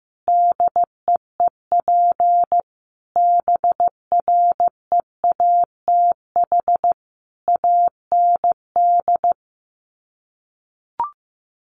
Morse code, heard as a keyed tone, about 15 wpm, 700 Hz. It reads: DEEP BREATH AND
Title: Sets of 5 Words Encore 15wpm